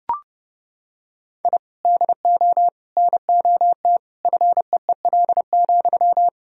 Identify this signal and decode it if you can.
Morse code, heard as a keyed tone, about 32 wpm, 700 Hz. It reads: I DO NOT FEEL,
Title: Sentences from Top 1000 Words - Review Part 3 - 30wpm